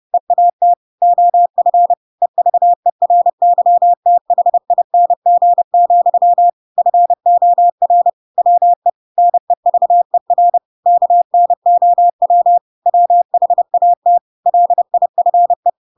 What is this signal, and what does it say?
Morse code, heard as a keyed tone, about 30 wpm, 700 Hz. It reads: EAT OF EVERYTHING, FOR WE NEVER KNOW WHAT LIFE
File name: Pinocchio / Pinocchio ch7 30 30 ik1hge